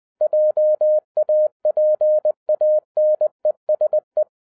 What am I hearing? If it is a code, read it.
Morse code, heard as a keyed tone, about 20 wpm, 610 Hz. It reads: JAPANESE